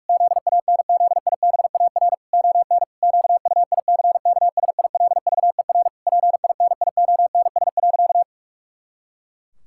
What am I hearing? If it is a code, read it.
Morse code, heard as a keyed tone, about 46 wpm, 700 Hz. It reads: ZANZIBAR ON QUICKSILVER PINIONS.
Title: Pangram Challenge v2 - 45wpm